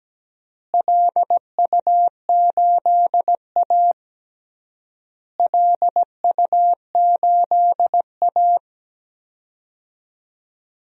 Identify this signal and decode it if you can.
Morse code, heard as a keyed tone, about 17 wpm, 700 Hz. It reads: LU8A LU8A
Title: DXCC-100 Call Signs - 4 Characters - Repeated 3x - 17wpm